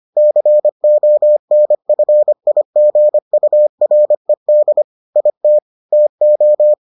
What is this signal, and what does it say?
Morse code, heard as a keyed tone, about 25 wpm, 600 Hz. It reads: CONFIGURED IT TO